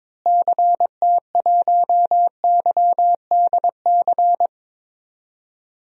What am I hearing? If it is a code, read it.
Morse code, heard as a keyed tone, about 21 wpm, 700 Hz. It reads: CT1YDC
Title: DXCC-100 Call Signs - 6 Characters - Repeated 3x - 22wpm